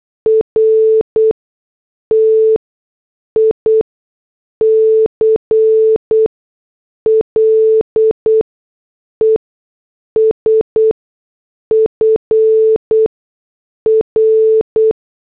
Morse code, heard as a keyed tone, about 8 wpm, spaced as for 6 wpm, 440 Hz. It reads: RTICLES FR
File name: morse / headlines